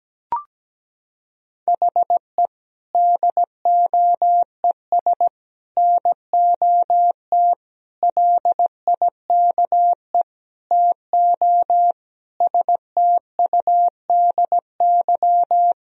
Morse code, heard as a keyed tone, about 17 wpm, 700 Hz. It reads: HE DOES NOT LIKE TO STUDY
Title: Sentences from Top 1000 Words - Review Part 3 - 17wpm